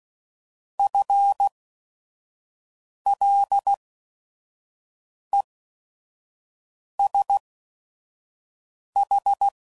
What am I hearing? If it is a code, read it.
Morse code, heard as a keyed tone, about 16 wpm, spaced as for 5 wpm, 800 Hz. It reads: FLESH
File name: Prefixes / By First Letter / Fs